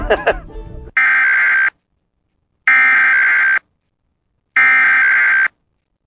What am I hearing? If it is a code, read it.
Emergency Alert System (SAME) tones (header bursts), heard as harsh buzzing bursts, with speech. It is heard from 1.0 to 5.5 s.